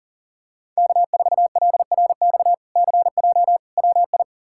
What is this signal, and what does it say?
Morse code, heard as a keyed tone, about 40 wpm, 700 Hz. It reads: K4LRX CJ WI
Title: K1USN SST QSO Practice - 40wpm